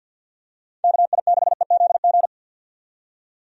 Morse code, heard as a keyed tone, about 50 wpm, 700 Hz. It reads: KI6EZG
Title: USA Call Signs - 6 Characters - 50wpm